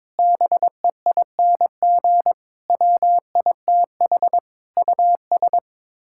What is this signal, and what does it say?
Morse code, heard as a keyed tone, about 21 wpm, 700 Hz. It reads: BEING WITH US
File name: Sets of 9 Words - 22wpm